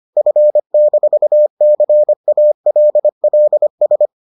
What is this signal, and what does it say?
Morse code, heard as a keyed tone, about 25 wpm, 600 Hz. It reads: F-CALLS